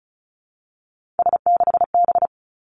Morse code, heard as a keyed tone, about 36 wpm, 700 Hz. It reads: S6B
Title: Three Challenging Letter-Number Combinations 35wpm